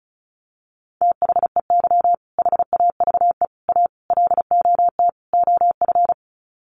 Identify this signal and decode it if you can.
Morse code, heard as a keyed tone, about 34 wpm, 700 Hz. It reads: THEY HAVE A LOT OF
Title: Sets of 5 Words Encore2 35wpm